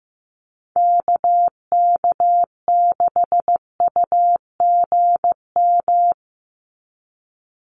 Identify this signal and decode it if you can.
Morse code, heard as a keyed tone, about 15 wpm, 700 Hz. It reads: KK6UGM
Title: USA Call Signs 15wpm